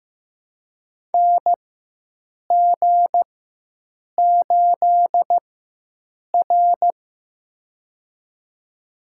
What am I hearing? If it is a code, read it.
Morse code, heard as a keyed tone, about 16 wpm, spaced as for 7 wpm, 700 Hz. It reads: NG8R
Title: USA Call Signs - 4 Characters - 4x Character Spacing - 15wpm